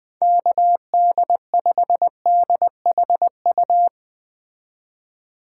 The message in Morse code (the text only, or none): KD5DHU